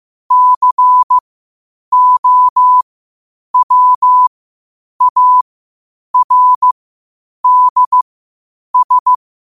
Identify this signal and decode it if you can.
Morse code, heard as a keyed tone, about 15 wpm, spaced as for 8 wpm, 1000 Hz. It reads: COWARDS
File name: qotdfiles / 8wpm